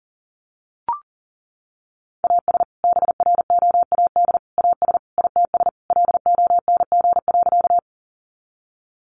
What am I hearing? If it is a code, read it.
Morse code, heard as a keyed tone, about 40 wpm, 700 Hz. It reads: AS BROAD AS ITS LONG.
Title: Sentences from Top 1000 Words 40wpm